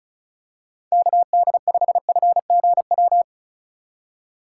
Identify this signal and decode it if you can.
Morse code, heard as a keyed tone, about 36 wpm, 700 Hz. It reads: KD5FGW
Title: USA Call Signs - 6 Characters - 8x Character Spacing - 35wpm